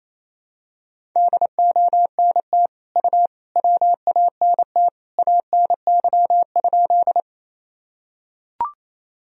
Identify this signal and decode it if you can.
Morse code, heard as a keyed tone, about 28 wpm, 700 Hz. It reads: DONT U WANT ANY?
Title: Sentences from Top 300 Words and Abbreviations 28wpm